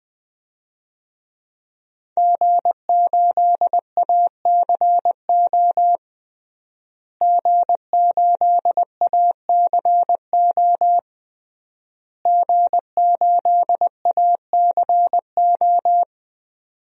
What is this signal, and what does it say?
Morse code, heard as a keyed tone, about 21 wpm, 700 Hz. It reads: G8ACO G8ACO G8ACO
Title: DXCC-100 Call Signs - 5 Characters - Repeated 3x - 20wpm